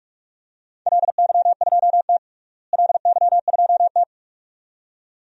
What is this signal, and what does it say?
Morse code, heard as a keyed tone, about 41 wpm, 700 Hz. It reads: LY2T LY2T
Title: DXCC-100 Call Signs - 4 Characters - Repeated 2x - 45wpm